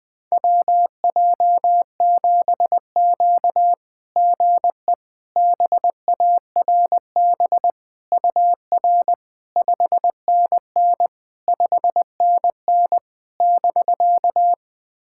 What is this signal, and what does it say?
Morse code, heard as a keyed tone, about 20 wpm, 690 Hz. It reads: WJ7Q GE BARB UR 5NN 5NN <BK>
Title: SOTA QSO Practice 20wpm